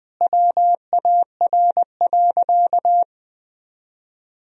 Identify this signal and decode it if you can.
Morse code, heard as a keyed tone, about 20 wpm, 700 Hz. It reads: WAR.
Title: Sentences from Top 1000 Words - Review Part 3 - 20wpm